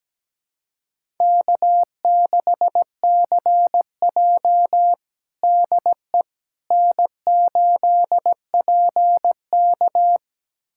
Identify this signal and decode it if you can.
Morse code, heard as a keyed tone, about 17 wpm, 700 Hz. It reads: K6CJ DE N8PK
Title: USA Callsign DE Callsign - 4 Characters - 17wpm